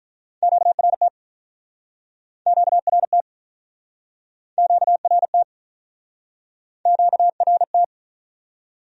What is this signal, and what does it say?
Morse code, heard as a keyed tone, about 47 wpm, 700 Hz. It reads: QRT QRT QRT QRT